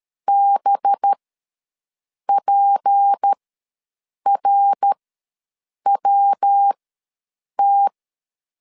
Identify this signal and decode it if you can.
Morse code, heard as a keyed tone, about 13 wpm, spaced as for 7 wpm, 800 Hz. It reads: BPRWT
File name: LEZIONI / 6 PKX0 / 6 P ws 12wpm 3117